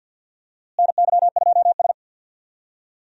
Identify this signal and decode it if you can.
Morse code, heard as a keyed tone, about 52 wpm, 700 Hz. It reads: NY2S
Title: USA Call Signs - 4 Characters - Repeated 2x - 50wpm